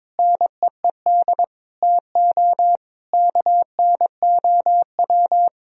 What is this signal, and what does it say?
Morse code, heard as a keyed tone, about 21 wpm, 700 Hz. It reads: NEED TO KNOW
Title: Sentences from Top 1000 Words - Review Part 3 - 22wpm